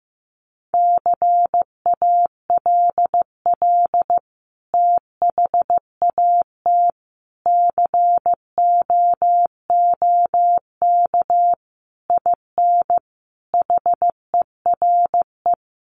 Morse code, heard as a keyed tone, about 15 wpm, 700 Hz. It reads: CALL THAT COOK IN HERE
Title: Sentences from Top 1100 Words 15wpm